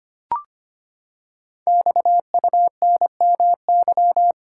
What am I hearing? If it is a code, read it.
Morse code, heard as a keyed tone, about 25 wpm, 700 Hz. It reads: XUNMY
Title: Code Groups - 5 Letters - 25wpm